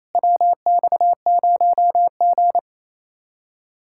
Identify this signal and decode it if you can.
Morse code, heard as a keyed tone, about 32 wpm, 700 Hz. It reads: WX0G
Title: Speed-Racing Common QSO Phrases - 28wpm